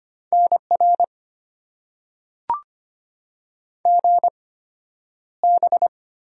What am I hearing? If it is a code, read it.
Morse code, heard as a keyed tone, about 25 wpm, spaced as for 11 wpm, 700 Hz. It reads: NR GB
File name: Top 90 QSO Elements - 8x Character Spacing - 25wpm